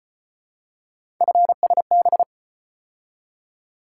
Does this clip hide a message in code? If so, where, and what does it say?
Yes: Morse code from 1.2 to 2.2 s, heard as a keyed tone, about 33 wpm, 700 Hz. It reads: FSB